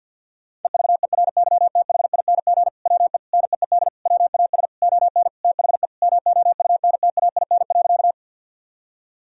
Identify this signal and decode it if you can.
Morse code, heard as a keyed tone, about 52 wpm, 700 Hz. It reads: EVERYTHING WE NEED WAS ON THE MOUNTAIN.
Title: Sentences from Top 1000 Words - Review Part 1 - 50wpm